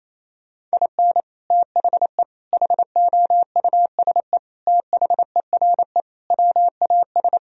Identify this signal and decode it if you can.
Morse code, heard as a keyed tone, about 27 wpm, 700 Hz. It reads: IN THE HOUSE THERE WAS
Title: Sentences from Top 1000 Words - Review Part 2 - 28wpm